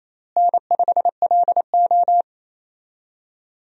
Morse code, heard as a keyed tone, about 28 wpm, 700 Hz. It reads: N5LO
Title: USA Call Signs - 4 Characters - Repeated 2x - 28wpm